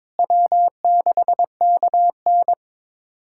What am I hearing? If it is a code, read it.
Morse code, heard as a keyed tone, about 22 wpm, 700 Hz. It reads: W6KN
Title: USA Callsign DE Callsign - 4 Characters - 22wpm